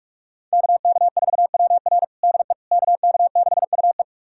Morse code, heard as a keyed tone, about 42 wpm, 700 Hz. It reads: KK4WR DE KK6UE
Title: USA Callsign DE Callsign - 5 Characters - 45wpm